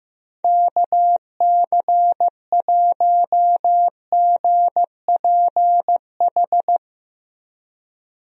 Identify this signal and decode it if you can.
Morse code, heard as a keyed tone, about 15 wpm, 700 Hz. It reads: KC1GPH
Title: USA Call Signs - 6 Characters - Repeated 3x - 15wpm